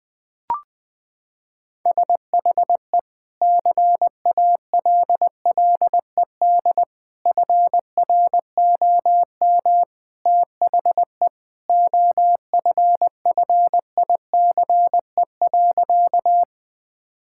Morse code, heard as a keyed tone, about 20 wpm, 700 Hz. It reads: SHE CALLED FROM THE OFFICE.
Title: Sentences with 2-Word Phrases - 20wpm